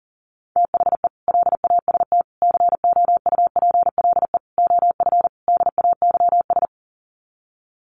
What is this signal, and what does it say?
Morse code, heard as a keyed tone, about 40 wpm, 700 Hz. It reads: THE LAST COUPLE OF DAYS
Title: Sets of 5 Words Encore 40wpm